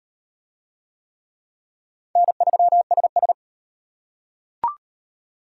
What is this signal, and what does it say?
Morse code, heard as a keyed tone, about 37 wpm, 700 Hz. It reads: N3SS